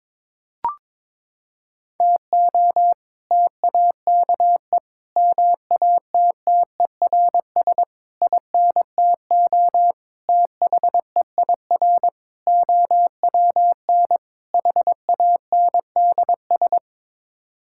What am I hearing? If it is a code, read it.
Morse code, heard as a keyed tone, about 21 wpm, 700 Hz. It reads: TO TAKE MATTERS INTO THEIR OWN HANDS
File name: practice / 22wpm / Sets of 7 Words - 22wpm